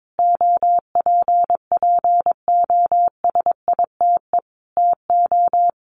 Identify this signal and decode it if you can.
Morse code, heard as a keyed tone, about 22 wpm, 700 Hz. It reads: OPPOSITE TO
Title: Sentences from Top 1300 Words 22wpm